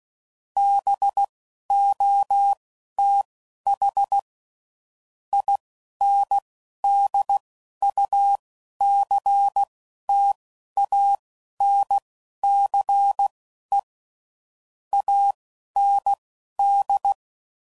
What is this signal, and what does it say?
Morse code, heard as a keyed tone, about 16 wpm, spaced as for 11 wpm, 800 Hz. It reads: BOTH INDUCTANCE AND
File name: CW Code Groups @10WPM / Technical Subjects / Reactance